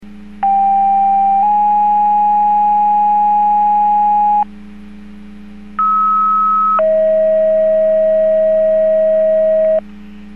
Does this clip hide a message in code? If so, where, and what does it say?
Yes: two-tone page from 0.4 to 9.8 s, heard as a one-second tone, then a three-second tone, about 790 then 830 Hz, about 1290 then 640 Hz.